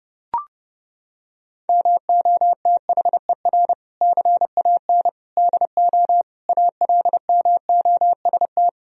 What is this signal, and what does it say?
Morse code, heard as a keyed tone, about 30 wpm, 700 Hz. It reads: MOTHER CAN DO ALMOST
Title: Sentences from Top 1000 Words - Review Part 4 - 30wpm